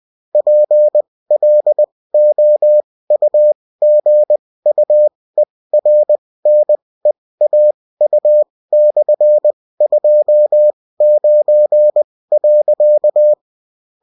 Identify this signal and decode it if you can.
Morse code, heard as a keyed tone, about 20 wpm, spaced as for 16 wpm, 600 Hz. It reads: PLOUGUERNEAU/29.